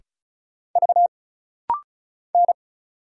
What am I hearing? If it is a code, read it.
Morse code, heard as a keyed tone, about 36 wpm, 700 Hz. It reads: V N
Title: Single Letter-Number Alt - Rapid-Fire 35wpm